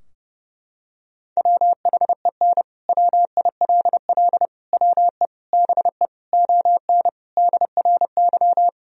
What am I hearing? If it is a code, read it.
Morse code, heard as a keyed tone, about 30 wpm, 700 Hz. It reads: WHEN WILL WE BE ON DRY
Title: Sentences from Top 1000 Words - Review Part 3 - 30wpm